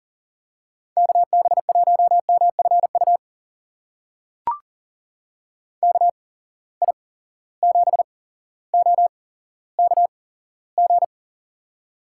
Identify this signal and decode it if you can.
Morse code, heard as a keyed tone, about 40 wpm, spaced as for 17 wpm, 700 Hz. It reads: KB1MFU KI7OKG
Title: USA Call Signs - 6 Characters - 8x Character Spacing - 40wpm